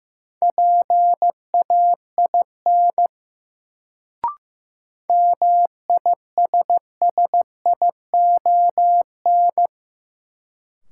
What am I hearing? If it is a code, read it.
Morse code, heard as a keyed tone, about 15 wpm, 700 Hz. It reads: PAIN MISSION
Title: Sentences from Top 1600 Words - New Word Drill - 15wpm